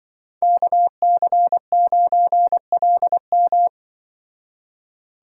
Morse code, heard as a keyed tone, about 24 wpm, 700 Hz. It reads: KC9LM
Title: Speed Racing - USA 2x2 Call Signs 20wpm